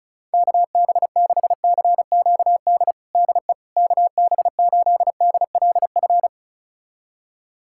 Morse code, heard as a keyed tone, about 36 wpm, 700 Hz. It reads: KB6CQD DE KB8DLF
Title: USA Callsign DE Callsign 35wpm